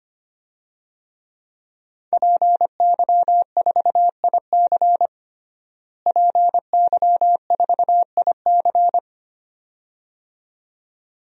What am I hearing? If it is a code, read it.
Morse code, heard as a keyed tone, about 25 wpm, 700 Hz. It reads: PY4IC PY4IC
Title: DXCC-100 Call Signs - 5 Characters - Repeated 2x - 25wpm